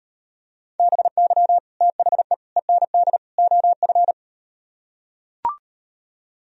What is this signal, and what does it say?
Morse code, heard as a keyed tone, about 38 wpm, 700 Hz. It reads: BY THE END OF